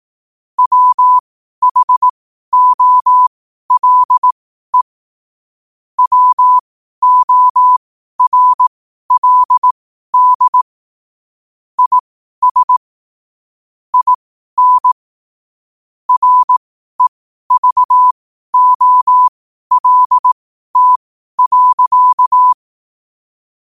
Morse code, heard as a keyed tone, about 18 wpm, spaced as for 12 wpm, 1000 Hz. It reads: WHOLE WORLD IS IN REVOLT.
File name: qotdfiles / 12wpm